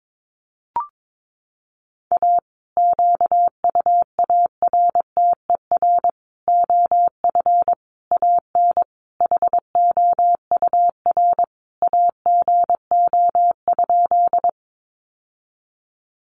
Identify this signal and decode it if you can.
Morse code, heard as a keyed tone, about 22 wpm, 700 Hz. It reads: A QUARTER OF AN HOUR AGO?
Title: Sentences from Top 1300 Words 22wpm